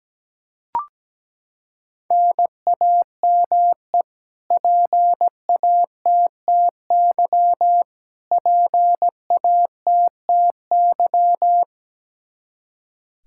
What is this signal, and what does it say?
Morse code, heard as a keyed tone, about 17 wpm, 700 Hz. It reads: NAME PATTY PATTY
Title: Practice QSO - Exchange 1 - Beginning - 17wpm